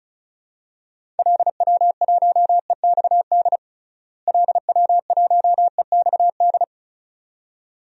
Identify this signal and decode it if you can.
Morse code, heard as a keyed tone, about 36 wpm, 700 Hz. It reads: LW1EXD LW1EXD
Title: DXCC-100 Call Signs - 6 Characters - Repeated 2x - 35wpm